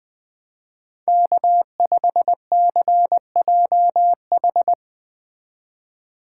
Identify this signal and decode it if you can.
Morse code, heard as a keyed tone, about 19 wpm, 700 Hz. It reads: K5CJH